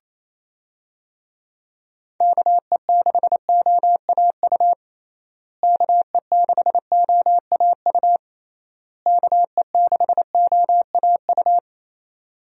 Morse code, heard as a keyed tone, about 28 wpm, 700 Hz. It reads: KE6OAU KE6OAU KE6OAU
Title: USA Call Signs - 6 Characters - Repeated 3x - 28wpm